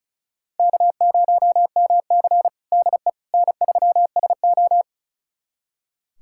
Morse code, heard as a keyed tone, about 36 wpm, 700 Hz. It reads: K0MC DE N3SO
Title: USA Callsign DE Callsign - 4 Characters - 35wpm